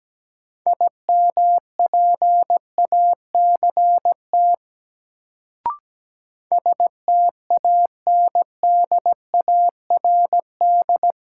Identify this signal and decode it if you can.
Morse code, heard as a keyed tone, about 17 wpm, 700 Hz. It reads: IMPACT STANDARD
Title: Sentences from Top 1000 Words - Review Part 6 - New Word Drill - 17wpm